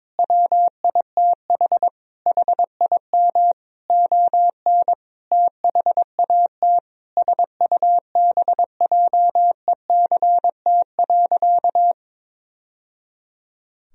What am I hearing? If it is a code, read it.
Morse code, heard as a keyed tone, about 22 wpm, 700 Hz. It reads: WITH HIM ON THAT SUBJECT.
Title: Sentences from Top 1000 Words - Review Part 4 - 22wpm